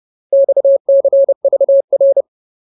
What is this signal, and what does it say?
Morse code, heard as a keyed tone, about 30 wpm, 550 Hz. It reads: XCVR
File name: Ultimate Word List / 30WPM Ultimate